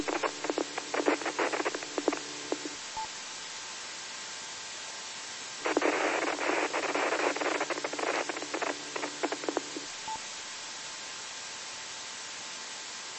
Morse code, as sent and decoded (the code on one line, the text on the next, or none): none